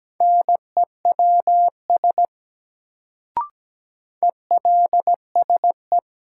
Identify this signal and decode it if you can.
Morse code, heard as a keyed tone, about 17 wpm, 700 Hz. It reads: NEWS ELSE
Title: Sentences from Top 800 Words - New Word Drill - 17wpm